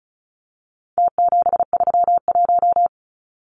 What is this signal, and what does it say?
Morse code, heard as a keyed tone, about 34 wpm, 700 Hz. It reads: T731
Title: Numbers - Contest Style - 4-Digit 35wpm